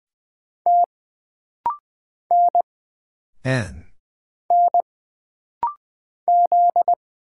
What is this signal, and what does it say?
Morse code, heard as a keyed tone, about 20 wpm, 700 Hz. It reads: T N N Z
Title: Single Letter-Number Alt - Rapid-Fire 20wpm